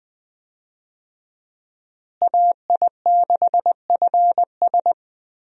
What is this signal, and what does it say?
Morse code, heard as a keyed tone, about 21 wpm, 700 Hz. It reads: AI6FS
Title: USA Call Signs - 5 Characters - Repeated 3x - 20wpm